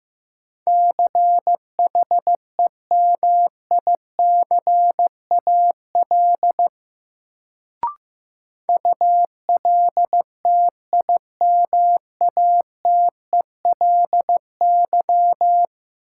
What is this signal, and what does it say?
Morse code, heard as a keyed tone, about 15 wpm, 700 Hz. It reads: CHEMICAL ULTIMATELY
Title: Sentences from Top 1200 Words - New Word Drill - 15wpm